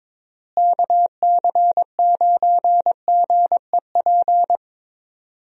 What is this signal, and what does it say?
Morse code, heard as a keyed tone, about 22 wpm, 700 Hz. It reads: KC9GEP
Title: USA Call Signs - 6 Characters - Repeated 2x - 22wpm